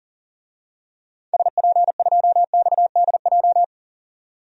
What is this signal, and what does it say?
Morse code, heard as a keyed tone, about 40 wpm, 700 Hz. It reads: SP2XDJ
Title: DXCC-100 Call Signs - 6 Characters - 40wpm